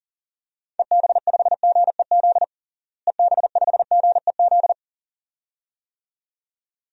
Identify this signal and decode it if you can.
Morse code, heard as a keyed tone, about 40 wpm, 700 Hz. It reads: EB5GEZ EB5GEZ
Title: DXCC-100 Call Signs - 6 Characters - Repeated 2x - 40wpm